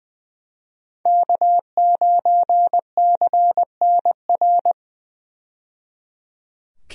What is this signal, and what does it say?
Morse code, heard as a keyed tone, about 20 wpm, 700 Hz. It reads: K9CNR
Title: USA Call Signs - 5 Characters - 20wpm